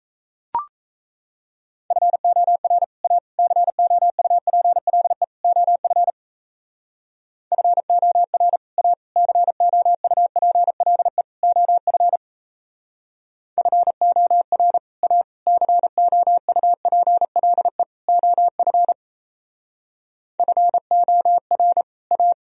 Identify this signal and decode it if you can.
Morse code, heard as a keyed tone, about 36 wpm, 700 Hz. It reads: FOR A COUPLE OF FOR A COUPLE OF FOR A COUPLE OF FOR A